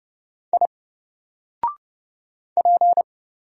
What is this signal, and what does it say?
Morse code, heard as a keyed tone, about 29 wpm, 700 Hz. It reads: I P
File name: Single-Letter-Number-30wpm